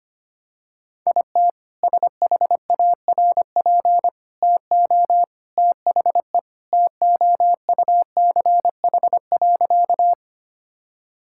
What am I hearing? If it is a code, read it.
Morse code, heard as a keyed tone, about 25 wpm, 700 Hz. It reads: IT SHARP TO THE TOUCH.
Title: Sentences from Top 1600 Words 25wpm